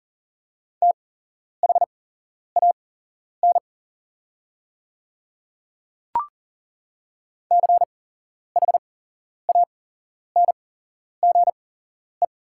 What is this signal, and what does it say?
Morse code, heard as a keyed tone, about 40 wpm, spaced as for 11 wpm, 690 Hz. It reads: THAN CHANGE
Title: 8x Character Spacing - Top 300 Words - 40wpm